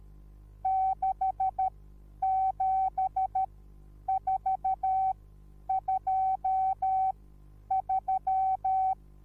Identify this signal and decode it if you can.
Morse code, heard as a keyed tone, about 13 wpm, spaced as for 9 wpm, 760 Hz. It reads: 67423
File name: Onlinelehrgang / morsen / DJ4UF Morsen21